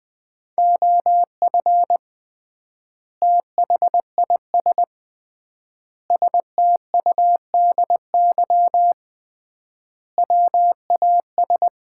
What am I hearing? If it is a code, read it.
Morse code, heard as a keyed tone, about 20 wpm, 700 Hz. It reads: OF THIS STUDY WAS
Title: 3x Word-Spacing - Sets of 5 Words - 20wpm